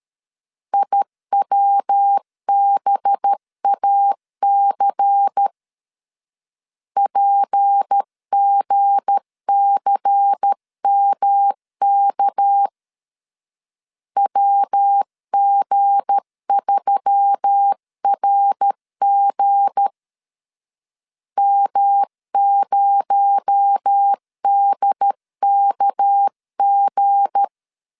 Morse code, heard as a keyed tone, about 13 wpm, 800 Hz. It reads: IWBAC PGCMK WG3RG M0DKG
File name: LEZIONI / 7 GF9Z / 7 G 12wpm 2055